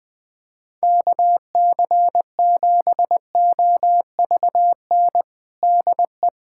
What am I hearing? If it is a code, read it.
Morse code, heard as a keyed tone, about 21 wpm, 700 Hz. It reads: KC7OVN DE
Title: USA Callsign DE Callsign 20wpm